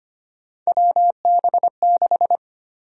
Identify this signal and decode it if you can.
Morse code, heard as a keyed tone, about 25 wpm, 700 Hz. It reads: WB6
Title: USA Call Sign Prefixes - 25wpm